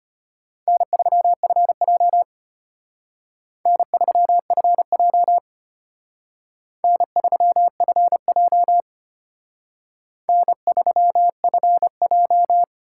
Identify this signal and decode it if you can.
Morse code, heard as a keyed tone, about 32 wpm, 700 Hz. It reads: N3FJ N3FJ N3FJ N3FJ